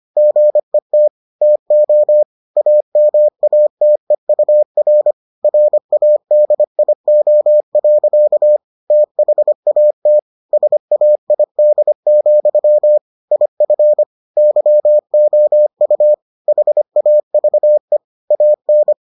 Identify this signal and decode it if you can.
Morse code, heard as a keyed tone, about 25 wpm, 600 Hz. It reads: GET TO AMATEUR RADIO. THAT SAID, IF YOU HAVE AN